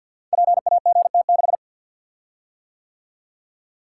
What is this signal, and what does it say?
Morse code, heard as a keyed tone, about 50 wpm, 700 Hz. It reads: PAGT6
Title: Code Groups - 5 Letter:Numbers - 50wpm